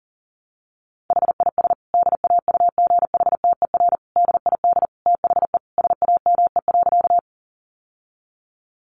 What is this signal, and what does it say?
Morse code, heard as a keyed tone, about 40 wpm, 700 Hz. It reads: HIS DAUGHTER DID THE SAME.
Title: Sentences from Top 1500 Words 40wpm